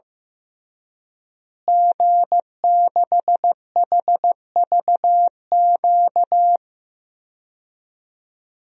Morse code, heard as a keyed tone, about 15 wpm, 700 Hz. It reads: G6HVQ
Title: DXCC-100 Call Signs - 5 Characters - Repeated 3x - 15wpm